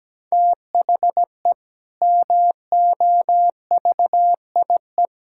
Morse code, heard as a keyed tone, about 17 wpm, 700 Hz. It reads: THE MOVIE
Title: Sentences from Top 1000 Words - Review Part 5 - 17wpm